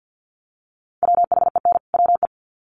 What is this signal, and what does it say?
Morse code, heard as a keyed tone, about 50 wpm, 700 Hz. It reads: WHEN WE
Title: Sets of 2 Words 50wpm